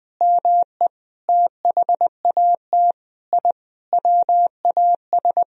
Morse code, heard as a keyed tone, about 20 wpm, 700 Hz. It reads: ME THAT I WAS
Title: Sentences from Top 1000 Words - Review Part 1 - 20wpm